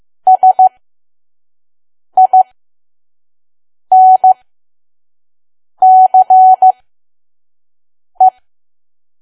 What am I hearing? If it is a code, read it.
Morse code, heard as a keyed tone, about 15 wpm, spaced as for 5 wpm, 750 Hz. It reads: SINCE
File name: Archive / 5 WPM files / PM